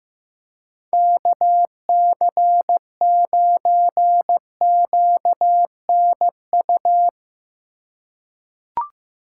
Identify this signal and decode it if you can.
Morse code, heard as a keyed tone, about 15 wpm, 700 Hz. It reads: KC9QNU
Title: USA Call Signs - 6 Characters - 2x Character Spacing - 15wpm